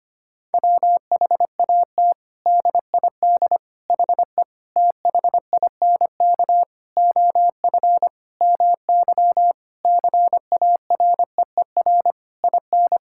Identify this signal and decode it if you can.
Morse code, heard as a keyed tone, about 25 wpm, 700 Hz. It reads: WHAT DID HE THINK OF MY CAREER IN
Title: Sentences from Top 1000 Words - Review Part 6 - 25wpm